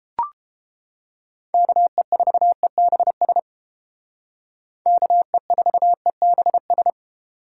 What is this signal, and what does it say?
Morse code, heard as a keyed tone, about 31 wpm, 700 Hz. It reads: KE4EBS KE4EBS